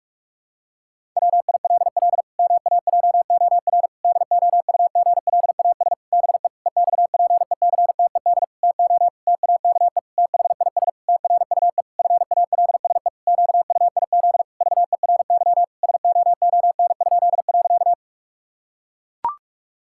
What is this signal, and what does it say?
Morse code, heard as a keyed tone, about 44 wpm, 700 Hz. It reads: WILL MAJOR DOUGLAS BE EXPECTED TO TAKE THIS TRUE FALSE QUIZ VERY SOON?.
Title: Pangram Challenge v2 - 45wpm